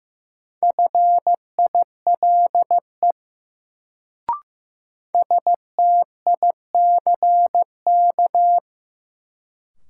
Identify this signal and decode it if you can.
Morse code, heard as a keyed tone, about 16 wpm, 700 Hz. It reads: FILE STICK
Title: Sentences from Top 1200 Words - New Word Drill Alt - 15wpm